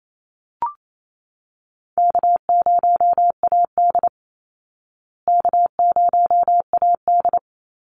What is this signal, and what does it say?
Morse code, heard as a keyed tone, about 27 wpm, 700 Hz. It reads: K0AD K0AD
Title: Easy Call Signs Repeated 2x - 28wpm